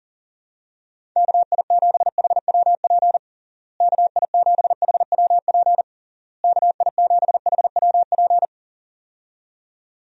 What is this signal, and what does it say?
Morse code, heard as a keyed tone, about 37 wpm, 700 Hz. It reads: KI7HWP KI7HWP KI7HWP
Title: USA Call Signs - 6 Characters - Repeated 3x - 40wpm